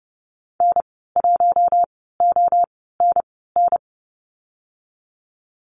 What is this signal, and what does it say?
Morse code, heard as a keyed tone, about 32 wpm, spaced as for 17 wpm, 700 Hz. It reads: N1ONN
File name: Lesson 19g - Letter w - Call Signs 30wpm